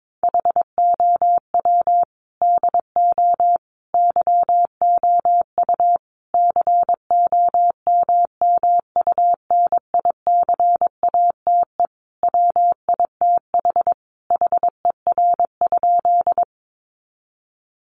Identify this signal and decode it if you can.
Morse code, heard as a keyed tone, about 22 wpm, 700 Hz. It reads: HOW DO YOU COMMUNICATE WITH HER?
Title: Sentences from Top 1300 Words 22wpm